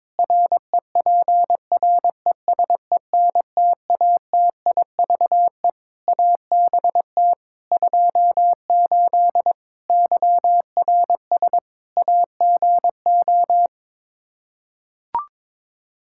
Morse code, heard as a keyed tone, about 22 wpm, 700 Hz. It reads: REPRESENTATIVE ABT 28 YRS AGO
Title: Practice QSO - Exchange 3 - More Info - 22wpm